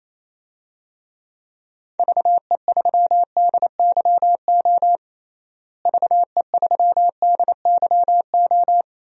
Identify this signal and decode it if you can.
Morse code, heard as a keyed tone, about 27 wpm, 700 Hz. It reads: VE3DYO VE3DYO
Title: DXCC-100 Call Signs - 6 Characters - Repeated 3x - 28wpm